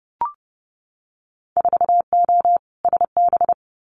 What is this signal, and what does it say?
Morse code, heard as a keyed tone, about 30 wpm, 700 Hz. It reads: 4O SB
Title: Winter Field Day Exchanges 30wpm